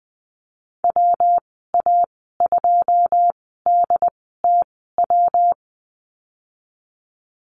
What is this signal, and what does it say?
Morse code, heard as a keyed tone, about 20 wpm, spaced as for 14 wpm, 700 Hz. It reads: WA2DTW
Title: 167 Lesson 27g - Number 7 - Call Signs 20wpm